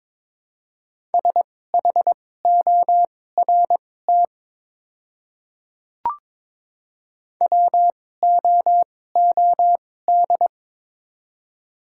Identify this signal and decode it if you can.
Morse code, heard as a keyed tone, about 22 wpm, spaced as for 16 wpm, 700 Hz. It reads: SHORT WOOD
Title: 2x Character Spacing - Top 500 Words - 22wpm